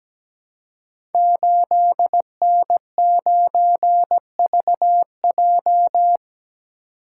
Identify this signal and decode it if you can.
Morse code, heard as a keyed tone, about 17 wpm, 700 Hz. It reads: 8N9VJ
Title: Code Groups - 5 Letter:Numbers - 17wpm